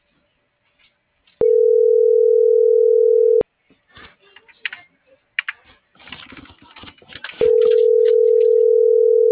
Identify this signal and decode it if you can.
Ringback tone (North America), 440 + 480 Hz.